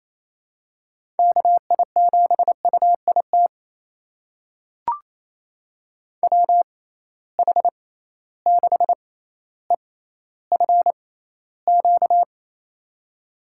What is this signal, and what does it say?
Morse code, heard as a keyed tone, about 28 wpm, spaced as for 14 wpm, 700 Hz. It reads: KI7UIT WH6EFQ
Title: USA Call Signs - 6 Characters - 6x Character Spacing - 28wpm